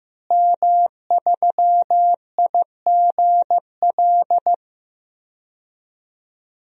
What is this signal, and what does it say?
Morse code, heard as a keyed tone, about 15 wpm, 700 Hz. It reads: M3IGL
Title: DXCC-100 Call Signs - 5 Characters - Repeated 2x - 15wpm